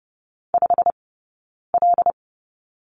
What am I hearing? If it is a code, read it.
Morse code, heard as a keyed tone, about 30 wpm, 700 Hz. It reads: 5 L
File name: Lesson 15d - Number 5 - Two Characters 30wpm